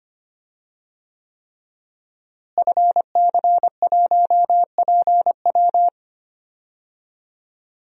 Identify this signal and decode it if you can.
Morse code, heard as a keyed tone, about 25 wpm, 700 Hz. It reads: FC1PW
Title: DXCC-100 Call Signs - 5 Characters - Repeated 3x - 25wpm